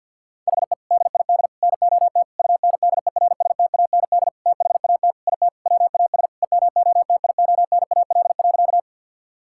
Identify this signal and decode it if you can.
Morse code, heard as a keyed tone, about 50 wpm, 700 Hz. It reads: HE DID NOT UNDERSTAND THAT IT WAS EMOTIONAL.
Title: Sentences from Top 1000 Words - Review Part 5 - 50wpm